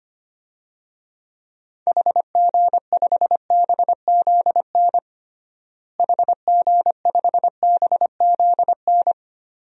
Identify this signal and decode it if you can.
Morse code, heard as a keyed tone, about 25 wpm, 700 Hz. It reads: HG5BZN HG5BZN
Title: DXCC-100 Call Signs - 6 Characters - Repeated 3x - 25wpm